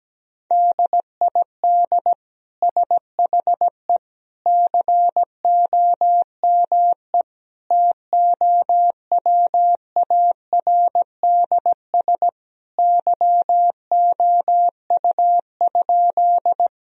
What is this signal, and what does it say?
Morse code, heard as a keyed tone, about 17 wpm, 700 Hz. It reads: DID SHE COME TOWARDS YOU?
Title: Sentences from Top 1000 Words - Review Part 2 - 17wpm